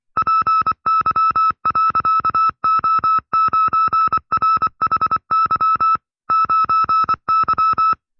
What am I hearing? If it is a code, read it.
Morse code, heard as a keyed tone, about 24 wpm, 1310 Hz. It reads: PY.O9RHY 9Y
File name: FreeFiles / CW-practice / 20WPM-side2